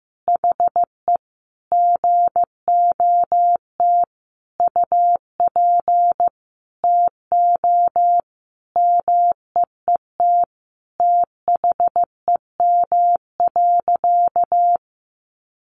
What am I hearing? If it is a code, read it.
Morse code, heard as a keyed tone, about 15 wpm, 700 Hz. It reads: HE GOT UP TO MEET THEM.
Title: Sentences from Top 1100 Words 15wpm